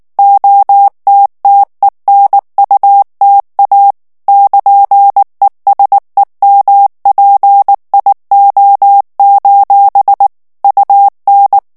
Morse code, heard as a keyed tone, about 19 wpm, 800 Hz. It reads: OTTENUTA <KN>ESEMPIO: UN
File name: morse mp3 / ITALIEN texte clair 2 vitesse 18 mots minute